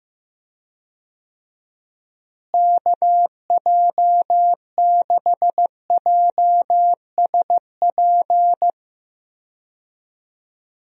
Morse code, heard as a keyed tone, about 15 wpm, 700 Hz. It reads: KJ6JSP
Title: USA Call Signs - 6 Characters - Repeated 2x - 15wpm